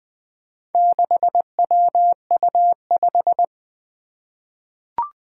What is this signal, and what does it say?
Morse code, heard as a keyed tone, about 21 wpm, 700 Hz. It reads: 6WU5
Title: Four Challenging Letter-Number Combinations 20wpm